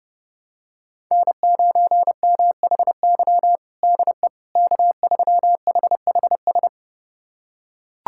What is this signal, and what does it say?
Morse code, heard as a keyed tone, about 32 wpm, 700 Hz. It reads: N9MHY DE K3HHS
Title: USA Callsign DE Callsign - 5 Characters - 30wpm